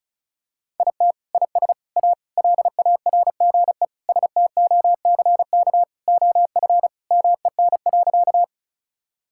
Morse code, heard as a keyed tone, about 36 wpm, 700 Hz. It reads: IT IS A LARGE STOCK OF MEN.
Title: Sentences from Top 1000 Words - Review Part 5 - 35wpm